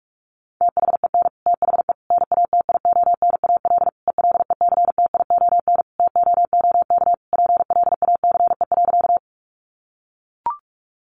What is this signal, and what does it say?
Morse code, heard as a keyed tone, about 44 wpm, 700 Hz. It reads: THEN THE NATIONAL ELECTION TOOK PLACE.
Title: Sentences from Top 1500 Words 45wpm